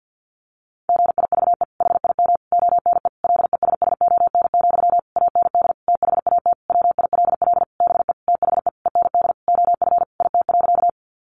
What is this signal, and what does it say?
Morse code, heard as a keyed tone, about 50 wpm, 700 Hz. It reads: GIVE HIM ONE LESSON, AND THAT WILL BE THE END OF IT.
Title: Sentences from Top 1200 Words 50wpm